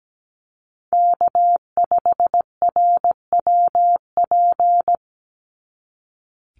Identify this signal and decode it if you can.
Morse code, heard as a keyed tone, about 17 wpm, 700 Hz. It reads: K5RWP
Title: NA SOTA Chaser Callsigns 17wpm